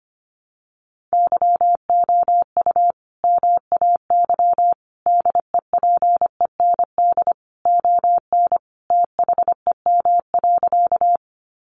Morse code, heard as a keyed tone, about 25 wpm, 700 Hz. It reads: YOU MAY DEPEND ON THEM.
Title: Sentences from Top 1300 Words 25wpm